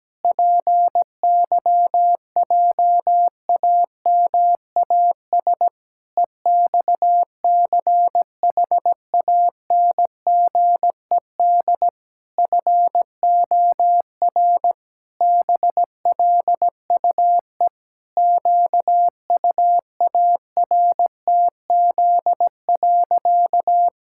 Morse code, heard as a keyed tone, about 17 wpm, 700 Hz. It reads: PYJAMAS EXCHANGED FOR BLUE QUARTZ.
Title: Pangram Challenge v2 - 17wpm